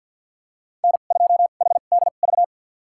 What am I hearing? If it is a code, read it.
Morse code, heard as a keyed tone, about 53 wpm, spaced as for 37 wpm, 700 Hz. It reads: N2HDV
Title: USA Call Signs - 5 Characters - 2x Character Spacing - 50wpm